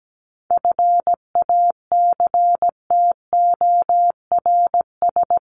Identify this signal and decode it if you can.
Morse code, heard as a keyed tone, about 17 wpm, 700 Hz. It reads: FACTORS
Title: ES:S Suffix - Top 100 Words with Base Words 17wpm